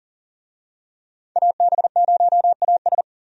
Morse code, heard as a keyed tone, about 40 wpm, 700 Hz. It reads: AB0AS
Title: USA Call Signs - 5 Characters - 40wpm